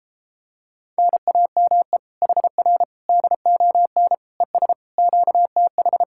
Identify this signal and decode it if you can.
Morse code, heard as a keyed tone, about 33 wpm, 700 Hz. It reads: NAME HR DON ES QTH